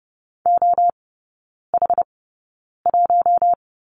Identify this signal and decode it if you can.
Morse code, heard as a keyed tone, about 30 wpm, spaced as for 9 wpm, 700 Hz. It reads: OH1
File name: Lesson 19e - Letter w - Three Characters 30wpm